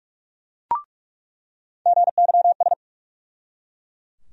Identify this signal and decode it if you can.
Morse code, heard as a keyed tone, about 43 wpm, 700 Hz. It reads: GYS
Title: USA Call Sign Postfixes - 3 Letters 45wpm